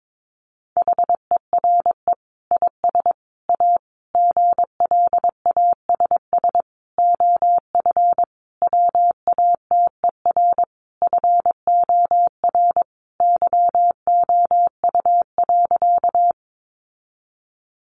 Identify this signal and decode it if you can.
Morse code, heard as a keyed tone, about 22 wpm, 700 Hz. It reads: HERE IS A GLASS OF WATER FOR YOU.
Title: Sentences from Top 1300 Words 22wpm